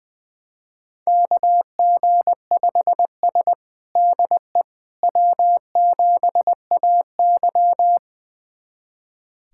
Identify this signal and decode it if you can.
Morse code, heard as a keyed tone, about 21 wpm, 700 Hz. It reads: KG5S DE W7AY
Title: USA Callsign DE Callsign - 4 Characters - 20wpm